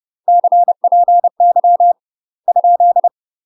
Morse code, heard as a keyed tone, about 30 wpm, 700 Hz. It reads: CPY ?
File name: UK MASTER QSOs TRAINING/30WPM